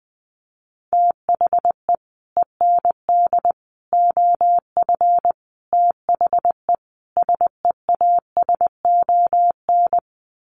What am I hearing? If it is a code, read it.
Morse code, heard as a keyed tone, about 20 wpm, 700 Hz. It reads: THE END OF THE SEASON
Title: Sets of 5 Words Encore 20wpm